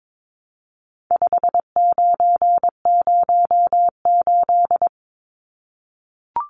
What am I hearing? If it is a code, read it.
Morse code, heard as a keyed tone, about 22 wpm, 700 Hz. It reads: 5908
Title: Numbers - 4-Digit 22wpm